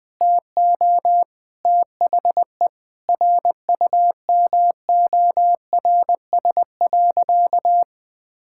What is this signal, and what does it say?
Morse code, heard as a keyed tone, about 20 wpm, 700 Hz. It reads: TO THE RUMORS.
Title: Sentences with 6-Word Phrases - 20wpm